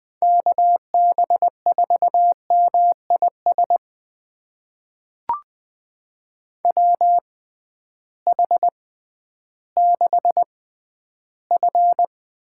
Morse code, heard as a keyed tone, about 21 wpm, spaced as for 12 wpm, 700 Hz. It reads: KB4MIS WH6F
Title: USA Call Signs - 6 Characters - 6x Character Spacing - 20wpm